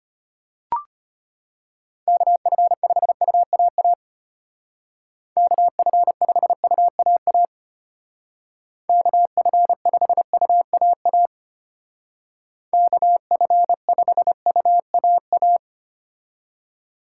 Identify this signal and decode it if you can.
Morse code, heard as a keyed tone, about 32 wpm, 700 Hz. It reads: KF5UAA KF5UAA KF5UAA KF5UAA